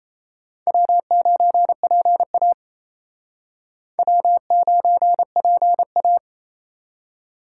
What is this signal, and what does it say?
Morse code, heard as a keyed tone, about 32 wpm, 700 Hz. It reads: W9PA W9PA
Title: Speed Racing - USA 1x2 Call Signs 28wpm